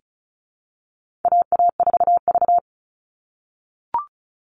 Morse code, heard as a keyed tone, about 34 wpm, 700 Hz. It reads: AA4V
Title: All Cut Numbers - 4-Digit 35wpm